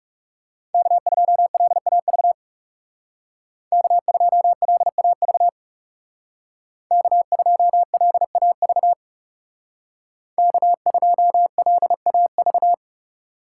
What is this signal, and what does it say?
Morse code, heard as a keyed tone, about 37 wpm, 700 Hz. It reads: K2LAV K2LAV K2LAV K2LAV
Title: Speed Racing - USA 1x3 Call Signs 30wpm